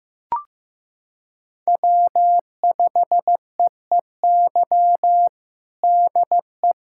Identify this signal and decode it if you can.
Morse code, heard as a keyed tone, about 15 wpm, 700 Hz. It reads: W5EEY DE
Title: USA Callsign DE Callsign - 5 Characters - 15wpm